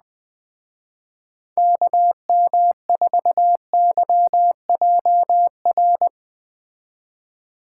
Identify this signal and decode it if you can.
Morse code, heard as a keyed tone, about 19 wpm, 700 Hz. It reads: KM4YJR
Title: Speed Racing - USA 2x3 Call Signs 17wpm